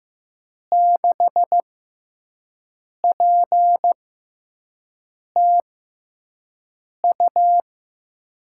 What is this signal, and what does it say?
Morse code, heard as a keyed tone, about 15 wpm, spaced as for 5 wpm, 700 Hz. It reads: 6PTU
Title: USA Call Signs - 6 Characters - 6x Character Spacing - 15wpm